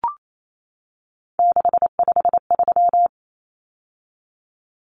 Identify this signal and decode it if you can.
Morse code, heard as a keyed tone, about 28 wpm, 700 Hz. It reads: 653
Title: Numbers - 3-Digit 28wpm